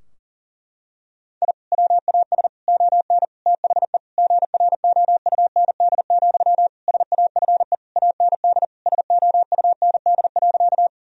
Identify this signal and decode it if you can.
Morse code, heard as a keyed tone, about 40 wpm, 700 Hz. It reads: I WAS ON THE GROUND, SAFE AND SOUND.
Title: Sentences from Top 1000 Words - Review Part 2 - 40wpm